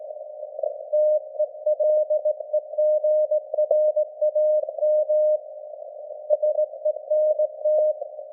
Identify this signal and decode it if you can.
Morse code, heard as a keyed tone, about 19 wpm, 610 Hz. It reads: TELEGRAM SENT
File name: laengstwellenempfang / SDRSharp 17kHz AF